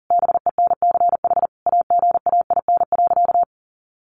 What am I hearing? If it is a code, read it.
Morse code, heard as a keyed tone, about 40 wpm, 700 Hz. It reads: BENCH AGAIN.
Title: Sentences from Top 1400 Words 40wpm